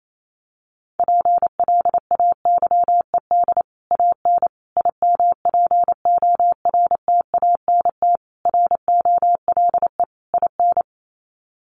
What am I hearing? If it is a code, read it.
Morse code, heard as a keyed tone, about 28 wpm, 700 Hz. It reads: PLAYED AN IMPORTANT ROLE IN
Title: Sets of 5 Words Encore2 28wpm